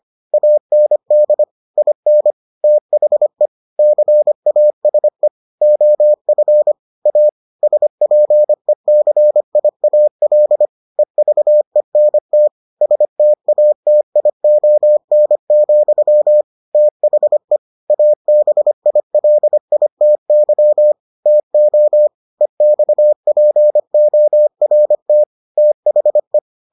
Morse code, heard as a keyed tone, about 25 wpm, 600 Hz. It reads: AND IN THE CASE OF A SPECIAL EVENT STATION, THE ABILITY TO EXPORT THE